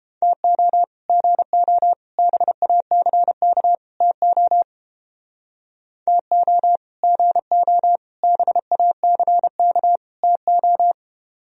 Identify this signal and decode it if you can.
Morse code, heard as a keyed tone, about 31 wpm, 700 Hz. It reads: TO GO BACK TO TO GO BACK TO